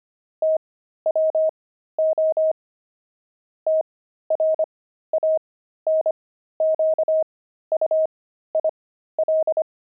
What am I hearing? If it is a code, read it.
Morse code, heard as a keyed tone, about 25 wpm, spaced as for 13 wpm, 630 Hz. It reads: TWO TRANQUIL
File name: pangram-fs-13-cs-25-aardvark